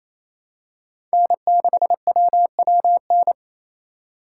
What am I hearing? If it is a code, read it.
Morse code, heard as a keyed tone, about 28 wpm, 700 Hz. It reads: N6WWN
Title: USA Call Signs - 5 Characters - 2x Character Spacing - 28wpm